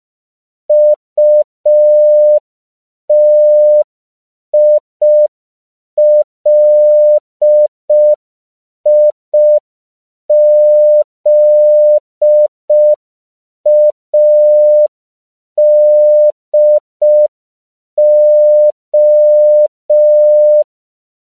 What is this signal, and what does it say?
Morse code, heard as a keyed tone, about 5 wpm, 600 Hz. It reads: UTILIZADO
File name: CW / 5PPM